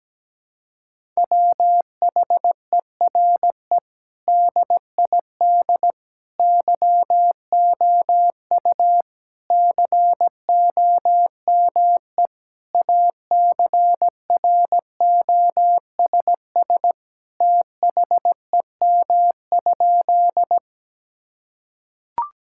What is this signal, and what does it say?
Morse code, heard as a keyed tone, about 17 wpm, 700 Hz. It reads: WHERE DID YOU COME ACROSS THEM?
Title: Sentences from Top 1000 Words - Review Part 5 - 17wpm